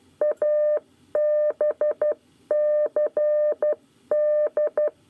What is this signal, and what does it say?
Morse code, heard as a keyed tone, about 12 wpm, 600 Hz. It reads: ABCD
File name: 9 lezione cw 1